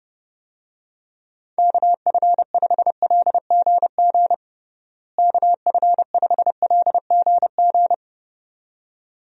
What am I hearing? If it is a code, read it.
Morse code, heard as a keyed tone, about 30 wpm, 700 Hz. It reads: KF5LGG KF5LGG
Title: USA Call Signs - 6 Characters - Repeated 2x - 30wpm